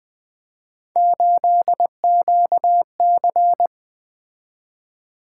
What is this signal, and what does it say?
Morse code, heard as a keyed tone, about 20 wpm, 700 Hz. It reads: 8QC
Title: Three Letter-Number Combinations - 20wpm